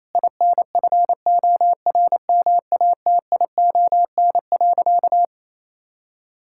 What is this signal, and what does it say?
Morse code, heard as a keyed tone, about 28 wpm, 700 Hz. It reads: INFORMATION.
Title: Sentences from Top 1000 Words - Review Part 6 - 28wpm